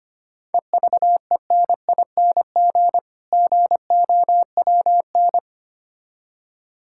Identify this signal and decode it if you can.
Morse code, heard as a keyed tone, about 25 wpm, 690 Hz. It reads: EVENING GOWN
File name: practice / 25wpm / Difficult Words with Hints - 25wpm